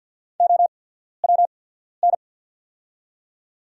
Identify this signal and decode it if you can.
Morse code, heard as a keyed tone, about 48 wpm, spaced as for 14 wpm, 690 Hz. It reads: OWN
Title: 8x Character Spacing - Top 300 Words - 50wpm